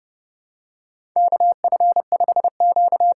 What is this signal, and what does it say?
Morse code, heard as a keyed tone, about 32 wpm, 700 Hz. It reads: KF5Q
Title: USA Call Signs - 6 Characters - 6x Character Spacing - 30wpm